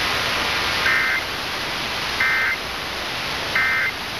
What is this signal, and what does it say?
Emergency Alert System (SAME) tones, heard as harsh buzzing bursts. It is heard from 0.7 to 4.0 s.